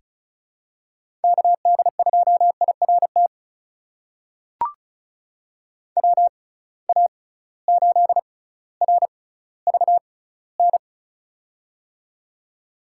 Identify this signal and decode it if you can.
Morse code, heard as a keyed tone, about 36 wpm, spaced as for 18 wpm, 700 Hz. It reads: KD2IRT WA8RVN
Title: USA Call Signs - 6 Characters - 6x Character Spacing - 35wpm